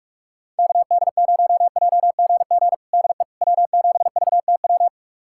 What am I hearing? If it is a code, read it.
Morse code, heard as a keyed tone, about 46 wpm, 700 Hz. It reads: KD0JGG DE W7VTW
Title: USA Callsign DE Callsign 45wpm